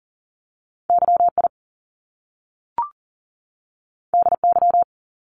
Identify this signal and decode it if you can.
Morse code, heard as a keyed tone, about 39 wpm, 700 Hz. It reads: YI DY
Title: Top Two Letter Combinations in Words 40wpm